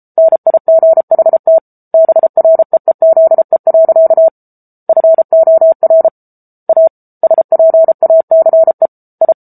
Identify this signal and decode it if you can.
Morse code, heard as a keyed tone, about 34 wpm, 650 Hz. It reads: NIGHT BREEZE. FOR A SPACE I
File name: Book Two / 07-Chapter 7 - 33 WPM